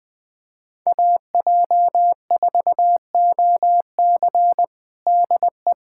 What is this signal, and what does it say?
Morse code, heard as a keyed tone, about 20 wpm, 700 Hz. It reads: AJ4OC DE
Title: K1USN SST QSO Practice 20wpm